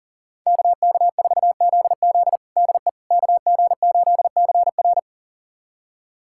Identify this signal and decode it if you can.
Morse code, heard as a keyed tone, about 40 wpm, 700 Hz. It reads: KK4ZZ DE KG8CR
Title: USA Callsign DE Callsign - 5 Characters - 40wpm